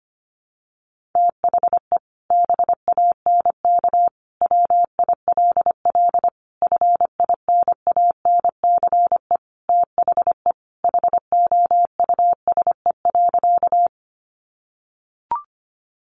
Morse code, heard as a keyed tone, about 25 wpm, 700 Hz. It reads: THE BANK WILL FINANCE THE HOUSE.
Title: Sentences from Top 1500 Words 25wpm